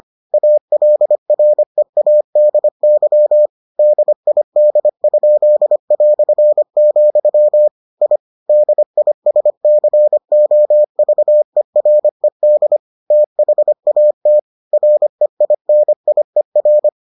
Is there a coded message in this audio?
Yes: Morse code from 0.3 to 16.9 s, heard as a keyed tone, about 25 wpm, 600 Hz. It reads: ALREADY DID?", I DISCOVERED THAT REINIER